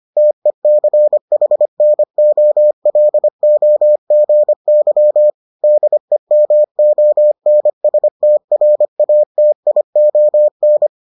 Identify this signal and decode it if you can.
Morse code, heard as a keyed tone, about 25 wpm, 600 Hz. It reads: TECHNOLOGY DEMONSTRATION